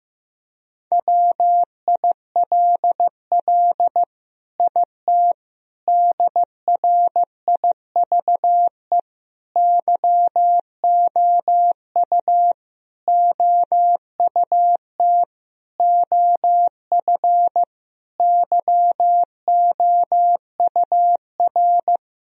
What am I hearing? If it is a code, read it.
Morse code, heard as a keyed tone, about 15 wpm, 700 Hz. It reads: WILL IT DRIVE YOU OUT OF YOUR
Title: Sentences from Top 1000 Words - Review Part 2 - 15wpm